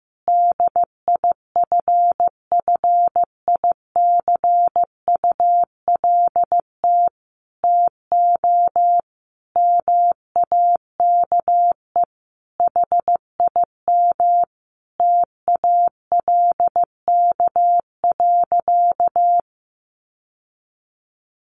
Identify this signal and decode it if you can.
Morse code, heard as a keyed tone, about 15 wpm, 700 Hz. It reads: DIFFICULT TO MAKE HIM TALK.
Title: Sentences from Top 600 Words 15wpm